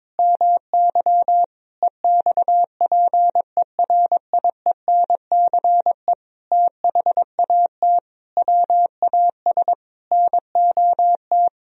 Morse code, heard as a keyed tone, about 22 wpm, 700 Hz. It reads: MY EXPERIENCE THAT WAS NOT
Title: Sentences from Top 1000 Words - Review Part 4 - 22wpm